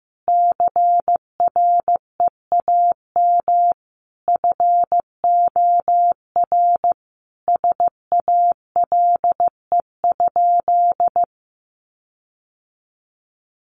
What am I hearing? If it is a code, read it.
Morse code, heard as a keyed tone, about 15 wpm, 700 Hz. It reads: CREAM FOR SALE?
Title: Sentences from Top 1500 Words 15wpm